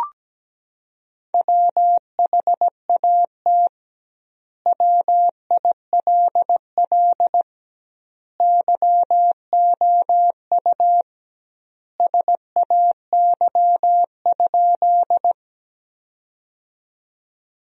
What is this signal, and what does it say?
Morse code, heard as a keyed tone, about 17 wpm, 700 Hz. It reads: WHAT WILL YOU SAY?
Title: 2x Word-Spacing - Sentences from Top 200 Words - 17wpm